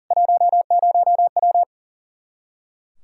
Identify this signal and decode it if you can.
Morse code, heard as a keyed tone, about 40 wpm, 700 Hz. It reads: 10W
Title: Practice QSO - Exchange 2 - Info - 40wpm